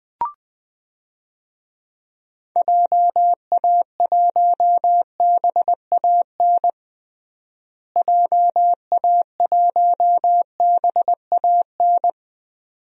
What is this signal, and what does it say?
Morse code, heard as a keyed tone, about 20 wpm, 700 Hz. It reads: JA1BAN JA1BAN
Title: DXCC-100 Call Signs - 6 Characters - Repeated 2x - 20wpm